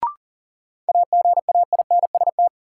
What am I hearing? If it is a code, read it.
Morse code, heard as a keyed tone, about 38 wpm, 700 Hz. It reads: AGAINST
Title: Sets of 2 Words - New Word Drill - 40wpm